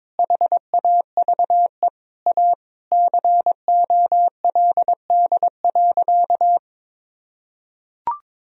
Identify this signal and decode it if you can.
Morse code, heard as a keyed tone, about 22 wpm, 700 Hz. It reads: HAVE A COLD.
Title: Sentences from Top 1000 Words - Review Part 3 - 22wpm